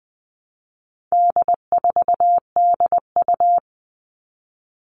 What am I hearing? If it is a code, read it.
Morse code, heard as a keyed tone, about 20 wpm, 700 Hz. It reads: D4DU
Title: All Cut Numbers - 4-Digit 20wpm